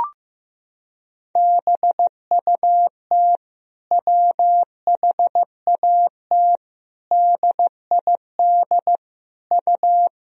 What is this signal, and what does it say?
Morse code, heard as a keyed tone, about 15 wpm, 700 Hz. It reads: BUT WHAT DID U
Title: Sentences from Top 100 Words and Abbreviations 15wpm